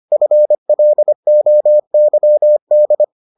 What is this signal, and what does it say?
Morse code, heard as a keyed tone, about 25 wpm, 600 Hz. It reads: FLOYD